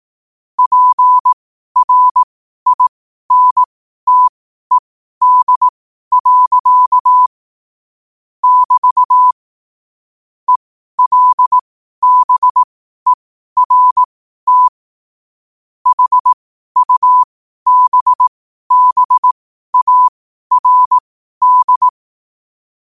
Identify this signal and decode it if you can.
Morse code, heard as a keyed tone, about 18 wpm, spaced as for 12 wpm, 1000 Hz. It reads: PRINTED. <BT> ELBERT HUBBARD
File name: qotdfiles / 12wpm